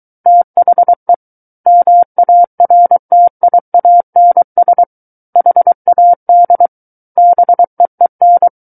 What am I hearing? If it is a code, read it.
Morse code, heard as a keyed tone, about 23 wpm, 700 Hz. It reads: THE MARTIANS HAD BEEN
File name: Book One / 14-Chapter 14 - 23 WPM